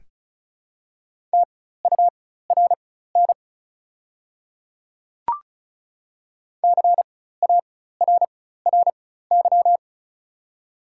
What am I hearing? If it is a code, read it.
Morse code, heard as a keyed tone, about 36 wpm, spaced as for 16 wpm, 700 Hz. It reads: TURN CARRY
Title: 4x Character Spacing - Top 300 Words - 35wpm